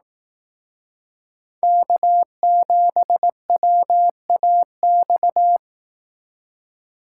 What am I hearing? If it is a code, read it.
Morse code, heard as a keyed tone, about 18 wpm, 700 Hz. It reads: K7WAX
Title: Speed Racing - USA 1x3 Call Signs 15wpm